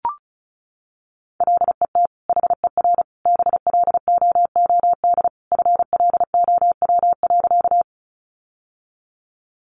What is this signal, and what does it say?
Morse code, heard as a keyed tone, about 34 wpm, 700 Hz. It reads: LET HER BLOOD FLOW.
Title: Sentences from Top 1200 Words 35wpm